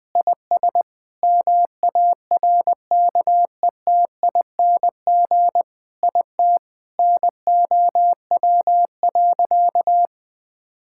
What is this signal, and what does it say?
Morse code, heard as a keyed tone, about 20 wpm, 700 Hz. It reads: IS MARKETING IT NOW.
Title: Sentences from Top 1000 Words - Review Part 5 - 20wpm